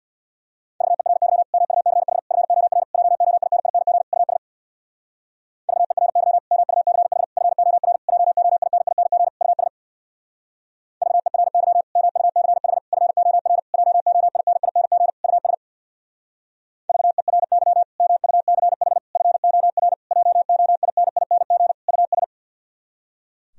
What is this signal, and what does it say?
Morse code, heard as a keyed tone, about 64 wpm, 700 Hz. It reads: VERY MUCH FOR JOINING US VERY MUCH FOR JOINING US VERY MUCH FOR JOINING US VERY MUCH FOR JOINING US